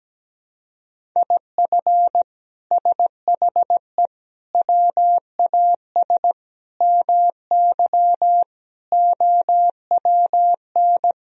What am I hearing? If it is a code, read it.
Morse code, heard as a keyed tone, about 17 wpm, 700 Hz. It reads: IF SHE WAS MY OWN
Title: Sentences from Top 1000 Words - Review Part 2 - 17wpm